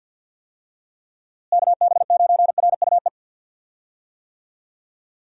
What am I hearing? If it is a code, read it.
Morse code, heard as a keyed tone, about 51 wpm, 700 Hz. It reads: KB9RUE
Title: USA Call Signs - 6 Characters - Repeated 3x - 50wpm